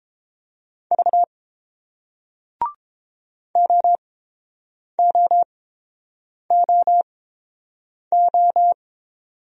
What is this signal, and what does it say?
Morse code, heard as a keyed tone, about 30 wpm, spaced as for 7 wpm, 700 Hz. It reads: V OOOO